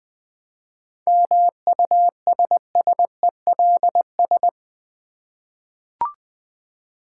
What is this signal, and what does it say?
Morse code, heard as a keyed tone, about 20 wpm, 700 Hz. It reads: MUSSELS
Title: Difficult Words with Hints - 20wpm